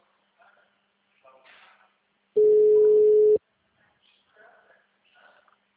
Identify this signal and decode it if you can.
Ringback tone (Europe, 425 Hz), with speech.